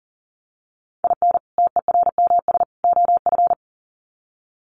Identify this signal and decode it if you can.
Morse code, heard as a keyed tone, about 38 wpm, 700 Hz. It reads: IN TERMS OF
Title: Sets of 3 Words 40wpm